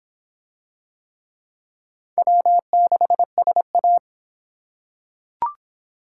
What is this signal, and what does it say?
Morse code, heard as a keyed tone, about 23 wpm, 700 Hz. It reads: W6SA